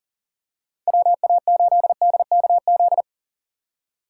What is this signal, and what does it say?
Morse code, heard as a keyed tone, about 40 wpm, 700 Hz. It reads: WA8DKZ
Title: USA Call Signs - 6 Characters - 3x Character Spacing - 40wpm